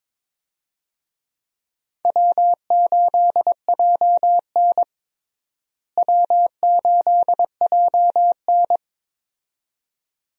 Morse code, heard as a keyed tone, about 22 wpm, 700 Hz. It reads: W8JN W8JN
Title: USA Call Signs - 4 Characters - Repeated 2x - 22wpm